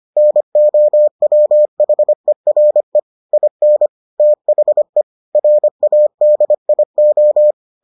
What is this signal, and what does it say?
Morse code, heard as a keyed tone, about 25 wpm, 600 Hz. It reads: NOWHERE IN THE RADIO